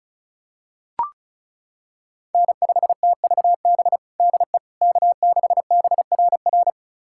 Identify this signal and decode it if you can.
Morse code, heard as a keyed tone, about 36 wpm, 700 Hz. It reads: N5TVB DE K6BRR
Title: USA Callsign DE Callsign - 5 Characters - 35wpm